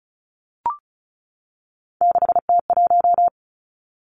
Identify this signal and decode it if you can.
Morse code, heard as a keyed tone, about 36 wpm, 700 Hz. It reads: 6T1
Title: Three Challenging Letter-Number Combinations 35wpm